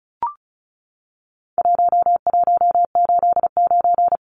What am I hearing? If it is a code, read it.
Morse code, heard as a keyed tone, about 34 wpm, 700 Hz. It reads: 1189
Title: Numbers - 4-Digit 35wpm